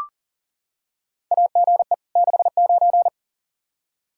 Morse code, heard as a keyed tone, about 39 wpm, 700 Hz. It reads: AGE 69
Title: Practice QSO - Exchange 3 - More Info - 40wpm